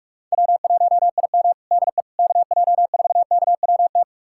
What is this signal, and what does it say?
Morse code, heard as a keyed tone, about 46 wpm, 700 Hz. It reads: W1IM DE KJ4KWT
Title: USA Callsign DE Callsign 45wpm